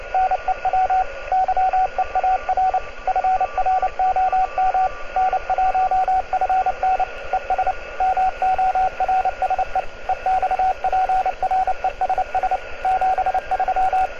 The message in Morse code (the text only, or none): NEW YEAR FROM N1FN ES MORSE EXPRESS 73